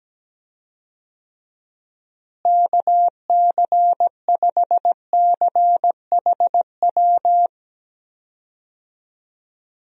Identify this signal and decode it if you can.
Morse code, heard as a keyed tone, about 17 wpm, 700 Hz. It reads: KC5CHW
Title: USA Call Signs - 6 Characters - Repeated 3x - 17wpm